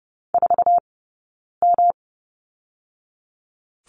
Morse code, heard as a keyed tone, about 30 wpm, 700 Hz. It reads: 4 M
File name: Lesson 20d - Number 3 - Two Characters 30wpm